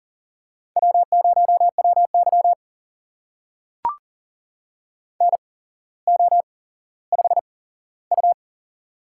Morse code, heard as a keyed tone, about 39 wpm, spaced as for 17 wpm, 700 Hz. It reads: W0WY NO5U
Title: USA Call Signs - 4 Characters - 8x Character Spacing - 40wpm